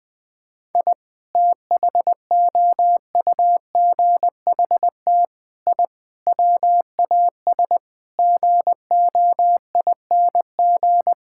Morse code, heard as a keyed tone, about 20 wpm, 700 Hz. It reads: I THOUGHT I WAS GOING
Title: Sentences with 6-Word Phrases - 20wpm